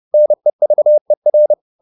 Morse code, heard as a keyed tone, about 30 wpm, 600 Hz. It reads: NEVER
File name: the old man and the sea / the old man and the sea 1 30wpm